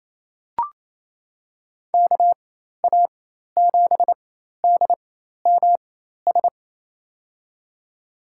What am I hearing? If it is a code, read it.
Morse code, heard as a keyed tone, about 28 wpm, spaced as for 13 wpm, 700 Hz. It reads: KA7DMS
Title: USA Call Signs - 6 Characters - 4x Character Spacing - 28wpm